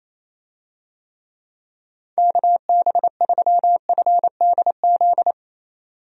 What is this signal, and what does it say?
Morse code, heard as a keyed tone, about 28 wpm, 700 Hz. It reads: KB3FDZ
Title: USA Call Signs - 6 Characters - 28wpm